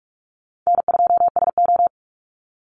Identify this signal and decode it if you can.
Morse code, heard as a keyed tone, about 44 wpm, 700 Hz. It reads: N2SO
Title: Easy Call Signs Repeated 3x - 45wpm